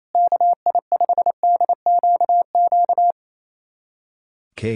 Morse code, heard as a keyed tone, about 28 wpm, 700 Hz. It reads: KI5DQQ
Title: USA Callsign DE Callsign - 6 Characters - 28wpm